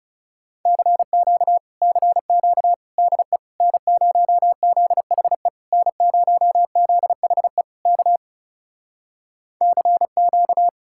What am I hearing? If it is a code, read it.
Morse code, heard as a keyed tone, about 40 wpm, 700 Hz. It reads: CQ CQ DE N0ZHE N0ZHE K CQ